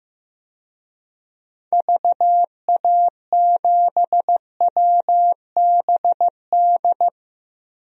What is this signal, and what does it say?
Morse code, heard as a keyed tone, about 15 wpm, 700 Hz. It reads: VA7WBD
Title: DXCC-100 Call Signs - 6 Characters - 15wpm